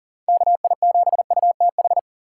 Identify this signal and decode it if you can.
Morse code, heard as a keyed tone, about 40 wpm, 700 Hz. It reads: KI7UTH
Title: USA Call Signs - 6 Characters - 2x Character Spacing - 40wpm